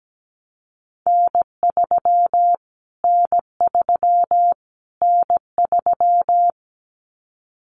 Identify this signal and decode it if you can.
Morse code, heard as a keyed tone, about 17 wpm, 700 Hz. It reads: N3 N3 N3
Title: Callsign Prefixes for Top 25 DXCC - Repeated 3x - 17wpm